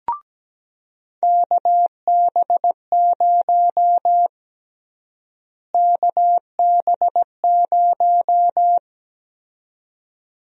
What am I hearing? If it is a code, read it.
Morse code, heard as a keyed tone, about 17 wpm, 700 Hz. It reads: KB0 KB0
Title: USA Call Sign Prefixes - Repeated 2x - 17wpm